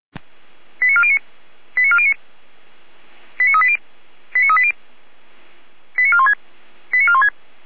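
Five-tone selective call (ZVEI), heard as quick runs of five tones, tones of about 70 ms. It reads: 89409 89409 89380 89380 89416 89416
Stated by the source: ZVEI five-tone call received on an SDR.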